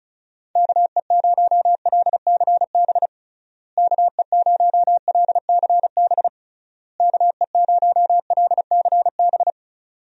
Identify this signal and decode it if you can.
Morse code, heard as a keyed tone, about 34 wpm, 700 Hz. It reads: KE0LCB KE0LCB KE0LCB
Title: USA Call Signs - 6 Characters - Repeated 3x - 35wpm